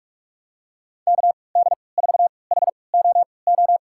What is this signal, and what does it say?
Morse code, heard as a keyed tone, about 46 wpm, spaced as for 26 wpm, 700 Hz. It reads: KD4HOO
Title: USA Call Signs - 6 Characters - 3x Character Spacing - 45wpm